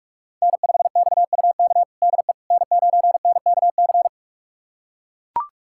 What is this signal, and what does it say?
Morse code, heard as a keyed tone, about 43 wpm, 700 Hz. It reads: N5XUK DE N9NKC
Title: USA Callsign DE Callsign - 5 Characters - 45wpm